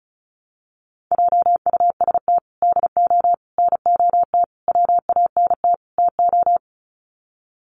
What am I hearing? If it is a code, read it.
Morse code, heard as a keyed tone, about 34 wpm, 700 Hz. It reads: JUST DO NOT WANT TO
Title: Sets of 5 Words Encore 35wpm